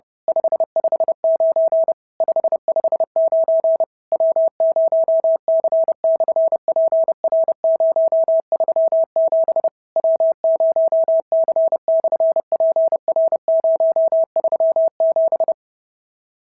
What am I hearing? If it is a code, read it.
Morse code, heard as a keyed tone, about 32 wpm, 690 Hz. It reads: 559 559 W0C/PR037 W0C/PR037
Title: SOTA QSO Practice 30wpm